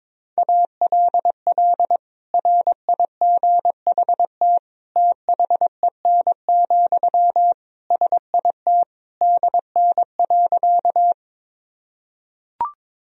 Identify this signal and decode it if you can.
Morse code, heard as a keyed tone, about 22 wpm, 700 Hz. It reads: ALL RIGHT THEN, SIT DN.
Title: Sentences from Top 500 Words and Abbreviations 22wpm